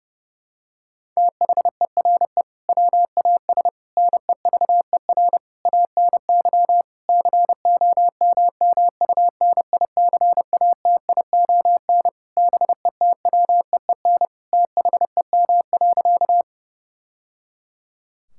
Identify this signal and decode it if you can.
Morse code, heard as a keyed tone, about 32 wpm, 700 Hz. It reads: THERE WAS NEVER ANY COMMUNICATION BETWEEN THEM.
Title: Sentences from Top 1000 Words - Review Part 6 - 30wpm